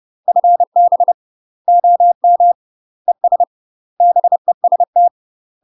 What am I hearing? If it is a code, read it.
Morse code, heard as a keyed tone, about 30 wpm, 700 Hz. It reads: FB OM ES BEST
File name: UK MASTER QSOs TRAINING / 30WPM / UK MASTER QSOs 0015